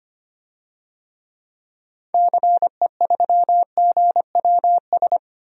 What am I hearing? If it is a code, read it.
Morse code, heard as a keyed tone, about 25 wpm, 700 Hz. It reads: CE3GWS
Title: DXCC-100 Call Signs - 6 Characters - 25wpm